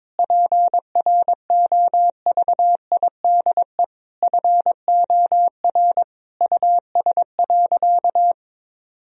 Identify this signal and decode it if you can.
Morse code, heard as a keyed tone, about 22 wpm, 700 Hz. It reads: PROVIDE FOR US.
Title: Sentences from Top 1000 Words - Review Part 3 - 22wpm